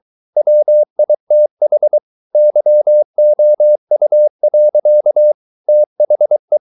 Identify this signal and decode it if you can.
Morse code, heard as a keyed tone, about 23 wpm, 600 Hz. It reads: WITH YOU. THE